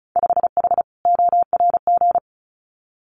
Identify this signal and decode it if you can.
Morse code, heard as a keyed tone, about 34 wpm, 700 Hz. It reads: 5H ORG
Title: Winter Field Day Exchanges Encore 35wpm